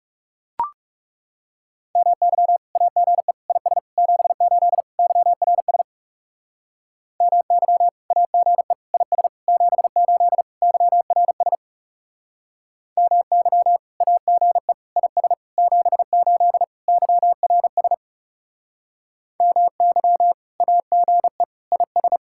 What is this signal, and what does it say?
Morse code, heard as a keyed tone, about 40 wpm, 700 Hz. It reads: MY AGE IS 78 YRS MY AGE IS 78 YRS MY AGE IS 78 YRS MY AGE IS